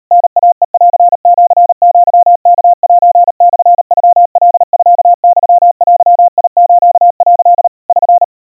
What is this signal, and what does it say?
Morse code (Japanese Wabun code), heard as a keyed tone, about 40 wpm, 700 Hz. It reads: タイヘンシアワセモノカミユデス」 <SN>
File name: kawa / Morse sample / kamyu 140 cpm 700 Hz Crispy